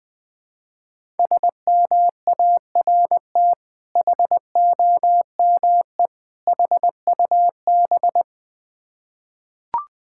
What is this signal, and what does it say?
Morse code, heard as a keyed tone, about 20 wpm, 690 Hz. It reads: SMART HOME HUB
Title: Difficult Words with Hints - 20wpm